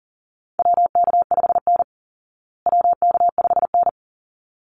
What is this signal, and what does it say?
Morse code, heard as a keyed tone, about 40 wpm, 700 Hz. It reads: WK5N WK5N
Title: Call Signs Repeated 3x - 40wpm